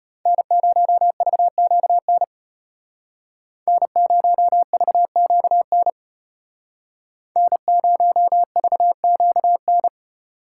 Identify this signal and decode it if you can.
Morse code, heard as a keyed tone, about 34 wpm, 700 Hz. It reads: N0VQN N0VQN N0VQN